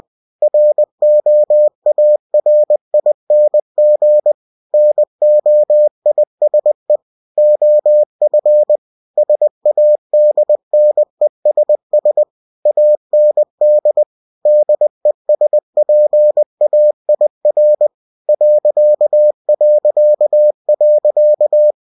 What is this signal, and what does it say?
Morse code, heard as a keyed tone, about 20 wpm, 600 Hz. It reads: ROARING NOISE OF SADNESS AND DESPAIR ...